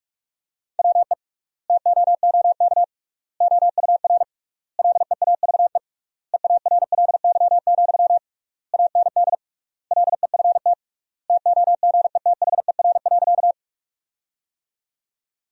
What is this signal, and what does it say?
Morse code, heard as a keyed tone, about 46 wpm, 700 Hz. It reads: WE TOOK OUR LEAVE EARLY, AND LEFT TOGETHER.
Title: 3x Word-Spacing - Sentences from Top 400 Words - 45wpm